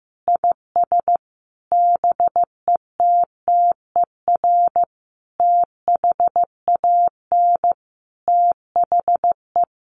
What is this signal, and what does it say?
Morse code, heard as a keyed tone, about 15 wpm, 700 Hz. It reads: IS BETTER THAN THE
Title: Sentences from Top 1400 Words 15wpm